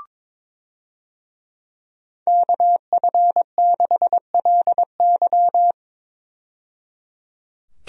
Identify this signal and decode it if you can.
Morse code, heard as a keyed tone, about 22 wpm, 700 Hz. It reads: KF6LY
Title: USA Call Signs - 5 Characters - 22wpm